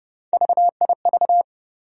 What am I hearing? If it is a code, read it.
Morse code, heard as a keyed tone, about 31 wpm, 700 Hz. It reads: VIV
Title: USA Call Sign Postfixes - 3 Letters 30wpm